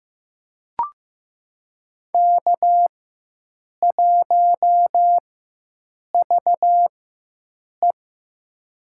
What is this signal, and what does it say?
Morse code, heard as a keyed tone, about 16 wpm, spaced as for 7 wpm, 700 Hz. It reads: K1VE
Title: USA Call Signs - 4x Character Spacing - 15wpm